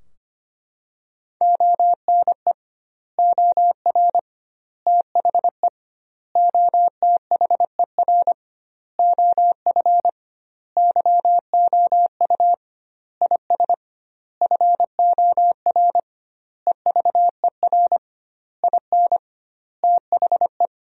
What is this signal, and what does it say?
Morse code, heard as a keyed tone, about 25 wpm, 700 Hz. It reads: ONE OR THE OTHER OF YOU IS FOR EVER IN THE
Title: 2x Word-Spacing - Sentences from Top 400 Words - 25wpm